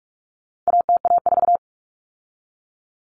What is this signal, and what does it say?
Morse code, heard as a keyed tone, about 44 wpm, 700 Hz. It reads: ATA4
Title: Common Cut Numbers - 4-Digit 45wpm